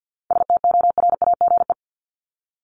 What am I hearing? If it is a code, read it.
Morse code, heard as a keyed tone, about 50 wpm, 700 Hz. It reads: STORAGE
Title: Top 1251-1500 Words 50wpm